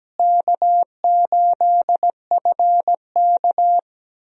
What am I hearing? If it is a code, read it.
Morse code, heard as a keyed tone, about 17 wpm, 700 Hz. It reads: K8FK
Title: USA Call Signs - 4 Characters - Repeated 2x - 17wpm